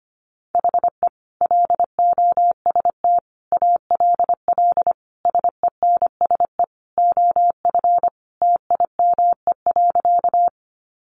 Morse code, heard as a keyed tone, about 25 wpm, 700 Hz. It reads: HE LOST ALL SENSE OF TIME.
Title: Sentences from Top 1400 Words 25wpm